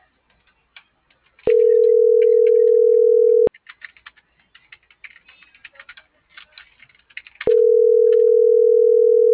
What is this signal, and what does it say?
Ringback tone (North America), 440 + 480 Hz.